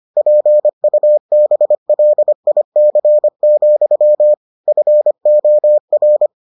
Morse code, heard as a keyed tone, about 25 wpm, 600 Hz. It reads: PUBLIC, FOR